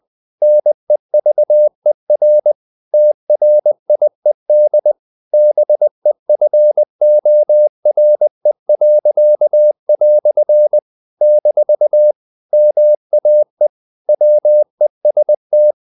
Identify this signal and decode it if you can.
Morse code, heard as a keyed tone, about 20 wpm, 600 Hz. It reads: NEVER TRIED BEFORE." - MAE WEST
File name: CW / Practice / EnglishQuotations0320